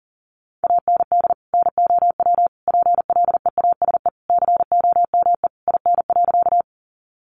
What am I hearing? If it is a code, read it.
Morse code, heard as a keyed tone, about 40 wpm, 700 Hz. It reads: AND NOW PLEASE COME IN.
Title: Sentences from Top 1500 Words 40wpm